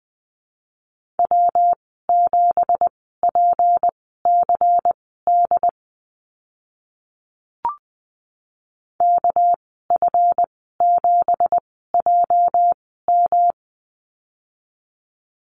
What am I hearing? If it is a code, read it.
Morse code, heard as a keyed tone, about 20 wpm, spaced as for 14 wpm, 700 Hz. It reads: W7PCD KF7JM
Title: 213 Lesson 34g - Number 8 - Call Signs 20wpm